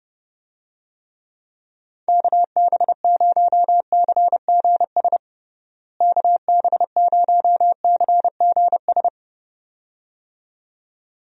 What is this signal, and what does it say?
Morse code, heard as a keyed tone, about 30 wpm, 700 Hz. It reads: KB0CGS KB0CGS
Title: USA Call Signs - 6 Characters - Repeated 2x - 30wpm